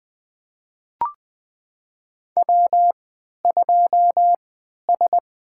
Morse code, heard as a keyed tone, about 20 wpm, spaced as for 11 wpm, 700 Hz. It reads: W2S
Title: USA Call Signs - 3x Character Spacing - 20wpm